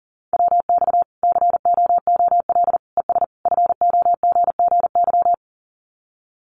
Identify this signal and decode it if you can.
Morse code, heard as a keyed tone, about 40 wpm, 700 Hz. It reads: WX COOL ES FOGGY
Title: Common QSO Phrases 40wpm